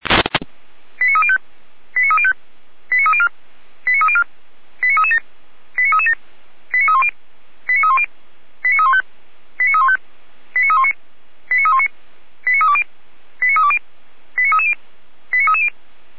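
Five-tone selective call (ZVEI), heard as quick runs of five tones, tones of about 80 ms. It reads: 89305 89305 89304 89304 89337 89337 89310 89310 89315 89315 89319 89319 89320 89320 89330 89330
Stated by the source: ZVEI five-tone selective call received by SDR.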